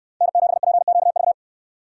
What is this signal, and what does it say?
Morse code, heard as a keyed tone, about 68 wpm, 700 Hz. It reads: N7PQV